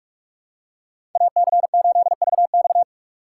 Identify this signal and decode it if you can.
Morse code, heard as a keyed tone, about 47 wpm, 700 Hz. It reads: AC8VX
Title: USA Call Signs - 5 Characters - 2x Character Spacing - 45wpm